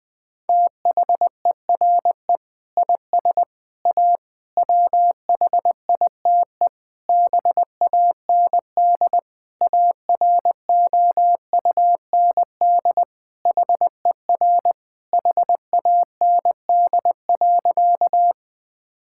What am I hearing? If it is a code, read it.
Morse code, heard as a keyed tone, about 20 wpm, 700 Hz. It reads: THERE IS A WHITE BAND AROUND HER HAND.
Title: Sentences from Top 1600 Words 20wpm